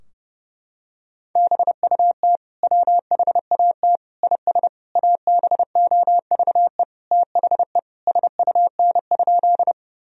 Morse code, heard as a keyed tone, about 30 wpm, 700 Hz. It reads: BUT WHAT IS ABOVE THE SUN?
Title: Sentences from Top 1000 Words - Review Part 2 - 30wpm